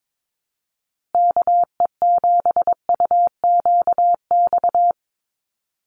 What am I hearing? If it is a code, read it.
Morse code, heard as a keyed tone, about 22 wpm, 700 Hz. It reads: KE7UQX
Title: USA Call Signs Repeated 2x 22wpm